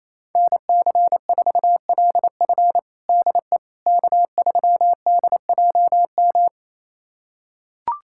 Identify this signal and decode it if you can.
Morse code, heard as a keyed tone, about 28 wpm, 700 Hz. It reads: NC4LF DE K3DJM
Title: USA Callsign DE Callsign - 5 Characters - 28wpm